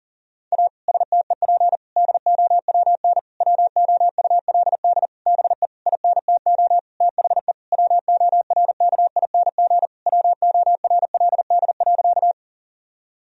Morse code, heard as a keyed tone, about 40 wpm, 700 Hz. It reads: A STEP DOWN WOULD BE INTO THE WORKING WORLD.
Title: Sentences from Top 1000 Words - Review Part 4 - 40wpm